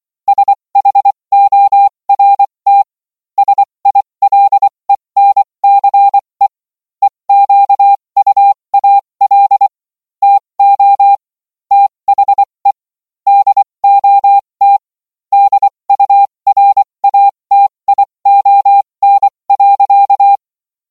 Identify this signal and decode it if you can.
Morse code, heard as a keyed tone, about 24 wpm, spaced as for 20 wpm, 800 Hz. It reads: SHORT SILENCE EQUAL TO THE DOT DURATION.